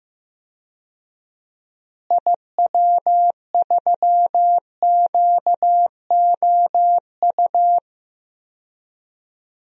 Morse code, heard as a keyed tone, about 15 wpm, 700 Hz. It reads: IW3QOU
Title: DXCC-100 Call Signs - 6 Characters - Repeated 3x - 15wpm